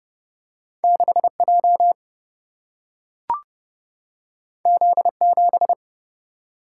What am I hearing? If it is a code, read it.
Morse code, heard as a keyed tone, about 29 wpm, 700 Hz. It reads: 6 J Z 7
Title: Single-Letter Single-Number 30wpm